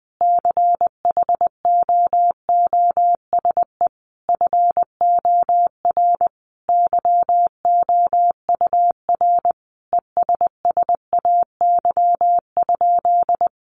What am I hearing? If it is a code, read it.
Morse code, heard as a keyed tone, about 20 wpm, 700 Hz. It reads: CHOOSE FOR YOUR ESSAY?
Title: Sentences from Top 1500 Words 20wpm